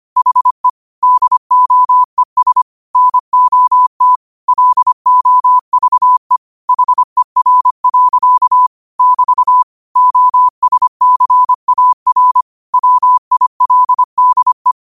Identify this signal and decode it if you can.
Morse code, heard as a keyed tone, about 25 wpm, 1000 Hz. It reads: HE DOES NOT LOVE HER. <BT> OSCAR WILDE
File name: qotdfiles / 25wpm